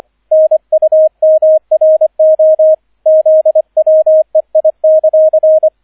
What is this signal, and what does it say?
Morse code, heard as a keyed tone, about 25 wpm, 630 Hz, with radio noise. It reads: NUMRO ZWEI;